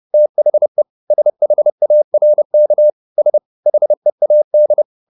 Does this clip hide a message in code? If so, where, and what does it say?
Yes: Morse code from 0.1 to 4.8 s, heard as a keyed tone, about 30 wpm, 600 Hz. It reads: THE SHARK S HEAD